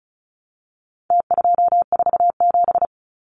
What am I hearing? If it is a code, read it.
Morse code, heard as a keyed tone, about 34 wpm, 700 Hz. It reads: T247
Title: Numbers - Contest Style - 4-Digit 35wpm